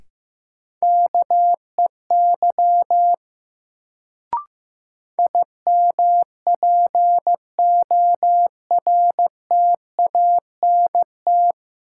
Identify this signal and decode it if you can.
Morse code, heard as a keyed tone, about 15 wpm, 700 Hz. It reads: KEY IMPORTANT
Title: Sentences from Top 600 Words - New Word Drill - 15wpm